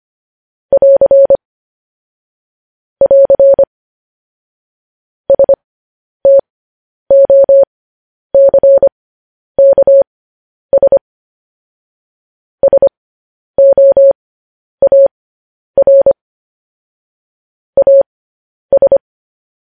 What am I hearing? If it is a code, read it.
Morse code, heard as a keyed tone, about 25 wpm, spaced as for 10 wpm, 550 Hz. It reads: <AR> <AR> STOCKS SOAR AS
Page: news headlines in morse code at 10 wpm